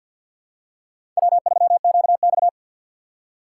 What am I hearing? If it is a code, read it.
Morse code, heard as a keyed tone, about 52 wpm, 700 Hz. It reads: W3QX
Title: USA Call Signs - 4 Characters - 2x Character Spacing - 50wpm